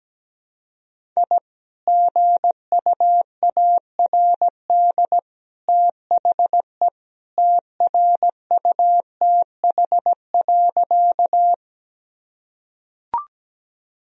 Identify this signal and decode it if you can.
Morse code, heard as a keyed tone, about 17 wpm, 700 Hz. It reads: I GUARD THE TRUTH.
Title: Sentences from Top 1000 Words - Review Part 4 - 17wpm